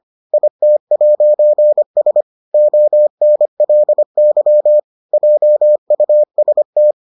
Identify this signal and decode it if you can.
Morse code, heard as a keyed tone, about 25 wpm, 600 Hz. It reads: IT'S ONLY JUST